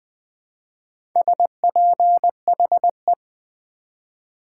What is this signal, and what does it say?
Morse code, heard as a keyed tone, about 20 wpm, 700 Hz. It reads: SPHE
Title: Build A Word - 2 Letters At A Time - 6-Letter Words - 20wpm